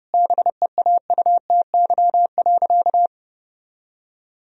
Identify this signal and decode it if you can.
Morse code, heard as a keyed tone, about 32 wpm, 700 Hz. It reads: BEAUTY.
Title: Sentences from Top 1000 Words - Review Part 2 - 30wpm